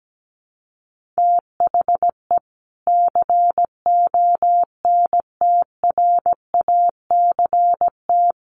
Morse code, heard as a keyed tone, about 17 wpm, 700 Hz. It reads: THE CONTRACT
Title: Sentences from Top 1400 Words 17wpm